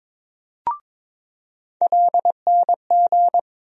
Morse code, heard as a keyed tone, about 22 wpm, 700 Hz. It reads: LNG
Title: USA Call Sign Postfixes - 3 Letters 22wpm